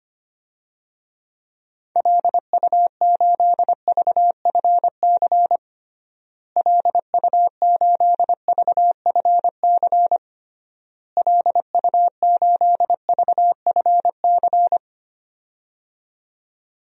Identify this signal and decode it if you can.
Morse code, heard as a keyed tone, about 25 wpm, 700 Hz. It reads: LU8VFC LU8VFC LU8VFC
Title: DXCC-100 Call Signs - 6 Characters - Repeated 3x - 25wpm